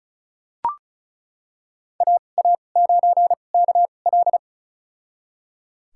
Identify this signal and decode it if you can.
Morse code, heard as a keyed tone, about 36 wpm, spaced as for 25 wpm, 700 Hz. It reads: AA9KL
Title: USA Call Signs - 5 Characters - 2x Character Spacing - 35wpm